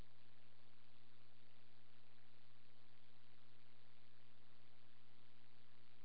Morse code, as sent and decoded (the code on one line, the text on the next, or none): none